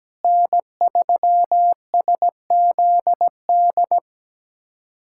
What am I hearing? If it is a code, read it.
Morse code, heard as a keyed tone, about 17 wpm, 700 Hz. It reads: N3SZD
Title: USA Callsign DE Callsign 17wpm